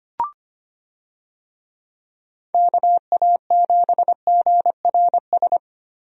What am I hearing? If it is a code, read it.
Morse code, heard as a keyed tone, about 25 wpm, 700 Hz. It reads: KA7GRS
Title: USA Call Signs - 6 Characters - 25wpm